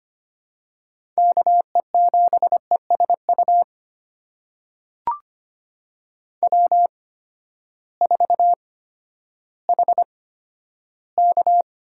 Morse code, heard as a keyed tone, about 25 wpm, spaced as for 12 wpm, 700 Hz. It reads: KE7ESU W4HK
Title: USA Call Signs - 8x Character Spacing - 25wpm